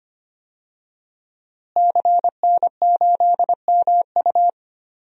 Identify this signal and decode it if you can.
Morse code, heard as a keyed tone, about 25 wpm, 700 Hz. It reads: CN8MU
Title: DXCC-100 Call Signs - 5 Characters - Repeated 3x - 25wpm